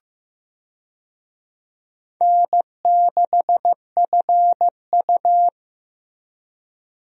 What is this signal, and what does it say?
Morse code, heard as a keyed tone, about 16 wpm, 700 Hz. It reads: N6FU
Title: USA Call Signs - 4 Characters - Repeated 3x - 15wpm